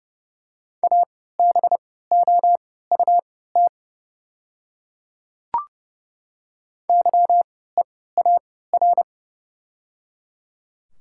Morse code, heard as a keyed tone, about 30 wpm, spaced as for 17 wpm, 700 Hz. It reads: ABOUT YEAR
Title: 3x Character Spacing - Top 200 Words - 30wpm